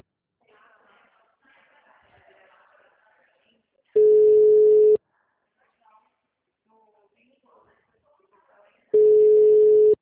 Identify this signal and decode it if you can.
Ringback tone (Europe, 425 Hz), with speech.
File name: out- -2024- - - .55362